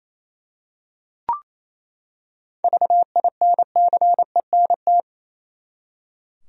Morse code, heard as a keyed tone, about 28 wpm, 700 Hz. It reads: VINCENT
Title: USA Common Names - 28wpm